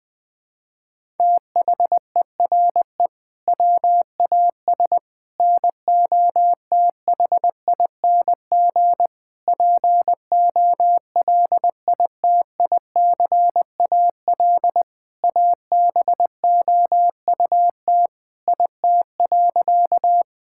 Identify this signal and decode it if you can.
Morse code, heard as a keyed tone, about 20 wpm, 700 Hz. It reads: THERE WAS NOTHING POLITICAL ABOUT IT.
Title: Sentences from Top 1000 Words - Review Part 5 - 20wpm